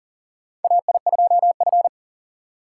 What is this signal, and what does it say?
Morse code, heard as a keyed tone, about 44 wpm, 700 Hz. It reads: AI2F
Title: USA Call Signs - 4 Characters - 6x Character Spacing - 40wpm